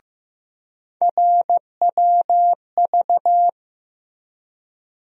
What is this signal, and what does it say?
Morse code, heard as a keyed tone, about 15 wpm, 700 Hz. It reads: RWV
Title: USA Call Sign Postfixes - 3 Letters 15wpm